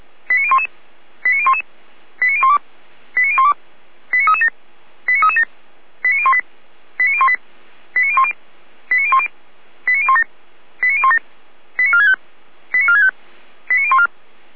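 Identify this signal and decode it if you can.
Five-tone selective call (ZVEI), heard as quick runs of five tones, tones of about 70 ms. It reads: 89011 89011 89012 89012 89337 89337 89018 89018 89010 89010 89017 89017 89465 89465 89014
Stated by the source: Received on SDR, ZVEI code 89320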